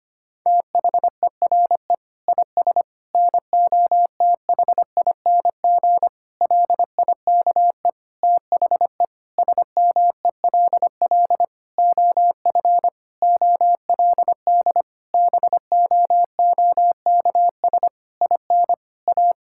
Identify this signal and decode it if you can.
Morse code, heard as a keyed tone, about 25 wpm, 700 Hz. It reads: THERE IS NOTHING LIKE THE SMELL OF OLD BOOKS IN A
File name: practice / 25wpm / Difficult Sentences with Hints - 25wpm